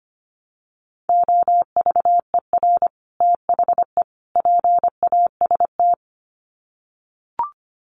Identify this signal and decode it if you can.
Morse code, heard as a keyed tone, about 25 wpm, 700 Hz. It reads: OVER THE PAST
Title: Sets of 3 Words 25wpm